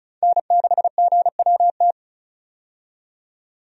Morse code, heard as a keyed tone, about 36 wpm, 700 Hz. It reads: N6GWT
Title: USA Call Signs - 5 Characters - 35wpm